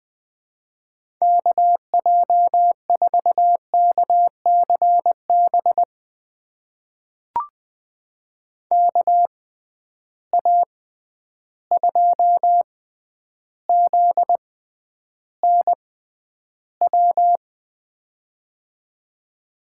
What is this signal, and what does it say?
Morse code, heard as a keyed tone, about 20 wpm, spaced as for 10 wpm, 700 Hz. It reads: KJ4KCB KA2ZNW
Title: USA Call Signs - 6 Characters - 6x Character Spacing - 20wpm